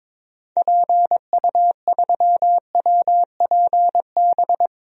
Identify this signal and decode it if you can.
Morse code, heard as a keyed tone, about 22 wpm, 700 Hz. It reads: PU3WPB
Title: DXCC-100 Call Signs - 6 Characters - Repeated 3x - 22wpm